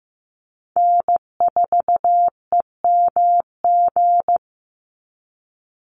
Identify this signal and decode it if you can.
Morse code, heard as a keyed tone, about 15 wpm, 700 Hz. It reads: N4EMG
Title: NA SOTA Chaser Callsigns 15wpm